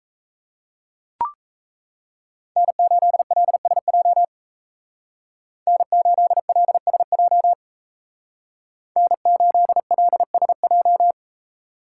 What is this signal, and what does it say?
Morse code, heard as a keyed tone, about 33 wpm, 700 Hz. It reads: N8LSJ N8LSJ N8LSJ
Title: Speed Racing - USA Call Signs 28wpm